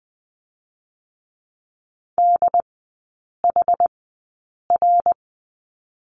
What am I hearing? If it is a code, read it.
Morse code, heard as a keyed tone, about 20 wpm, 700 Hz. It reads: D H R
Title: 100 Lesson 17e - Letter u - Three Characters 20wpm